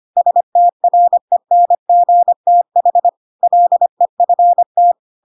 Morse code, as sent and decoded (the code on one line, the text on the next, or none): ... - .-. -. --. - .... / .-.. ..-. -
STRENGTH LEFT